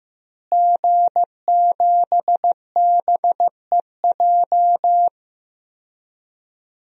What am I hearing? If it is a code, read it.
Morse code, heard as a keyed tone, about 15 wpm, 700 Hz. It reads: G7BEJ
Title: DXCC-100 Call Signs - 5 Characters - Repeated 2x - 15wpm